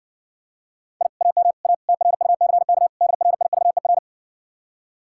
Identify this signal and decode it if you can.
Morse code, heard as a keyed tone, about 59 wpm, 700 Hz. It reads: I AM A TRUCK DRIVER